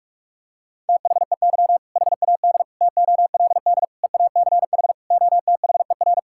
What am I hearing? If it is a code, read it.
Morse code, heard as a keyed tone, about 46 wpm, 700 Hz. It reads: THEY HAD TOLD EACH OTHER
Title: Sentences from Top 1600 Words 45wpm